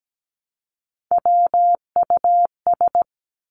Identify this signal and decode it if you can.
Morse code, heard as a keyed tone, about 17 wpm, 700 Hz. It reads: WUS
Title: Three Challenging Letter-Number Combinations 17wpm